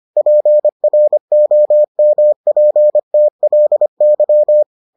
Morse code, heard as a keyed tone, about 25 wpm, 600 Hz. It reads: PROMPTLY